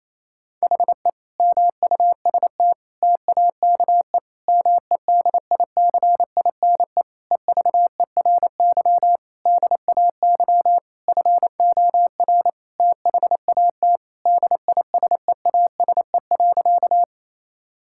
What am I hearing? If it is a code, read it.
Morse code, heard as a keyed tone, about 28 wpm, 700 Hz. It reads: HE MUST TAKE MEDICINE EVERY DAY FOR THAT DISEASE.
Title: Sentences from Top 1000 Words - Review Part 6 - 28wpm